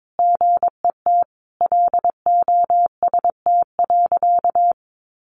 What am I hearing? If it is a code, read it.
Morse code, heard as a keyed tone, about 22 wpm, 700 Hz. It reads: GET LOST.
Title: Sentences from Top 1400 Words 22wpm